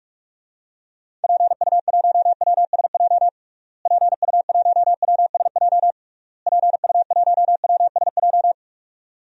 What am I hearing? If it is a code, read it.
Morse code, heard as a keyed tone, about 46 wpm, 700 Hz. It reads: PU1WSJ PU1WSJ PU1WSJ
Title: DXCC-100 Call Signs - 6 Characters - Repeated 3x - 45wpm